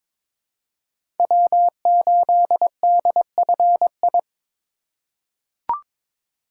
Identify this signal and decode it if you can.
Morse code, heard as a keyed tone, about 22 wpm, 690 Hz. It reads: W8DFI
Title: USA Call Signs - 5 Characters - 2x Character Spacing - 22wpm